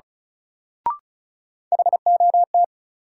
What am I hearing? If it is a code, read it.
Morse code, heard as a keyed tone, about 36 wpm, 700 Hz. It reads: HOT
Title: Sentences from Top 500 Words - New Word Drill - 35wpm